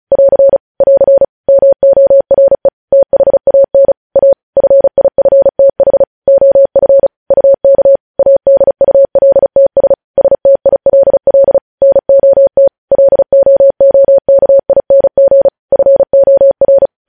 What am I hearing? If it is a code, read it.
Morse code, heard as a keyed tone, about 36 wpm, 550 Hz. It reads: <AR> <AR> MORE THAN A FIFTH OF UK ADULTS STILL NOT LOOKING FOR
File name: podcast / News headlines-1x-35x35